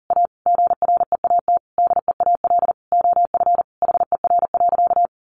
Morse code, heard as a keyed tone, about 40 wpm, 700 Hz. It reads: A GREAT DEAL OF HER.
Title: Sentences from Top 800 Words 40wpm